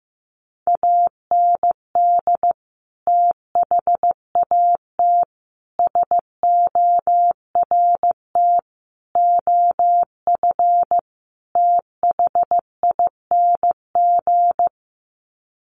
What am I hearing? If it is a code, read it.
Morse code, heard as a keyed tone, about 15 wpm, 700 Hz. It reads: AND THAT SORT OF THING
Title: Sets of 5 Words Encore2 15wpm